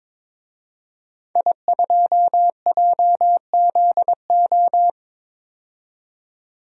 Morse code, heard as a keyed tone, about 22 wpm, 700 Hz. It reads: I2JZO
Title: DXCC-100 Call Signs - 5 Characters - 22wpm